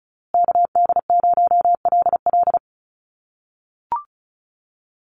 Morse code, heard as a keyed tone, about 34 wpm, 700 Hz. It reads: KD0LL
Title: USA Call Signs Repeated 3x 35wpm